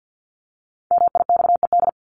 Morse code, heard as a keyed tone, about 50 wpm, 700 Hz. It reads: MIXED
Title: Top 1251-1500 Words 50wpm